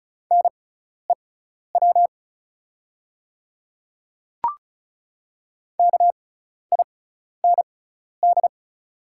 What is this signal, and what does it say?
Morse code, heard as a keyed tone, about 36 wpm, spaced as for 12 wpm, 690 Hz. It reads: NEW KIND
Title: 6x Character Spacing - Top 200 Words - 35wpm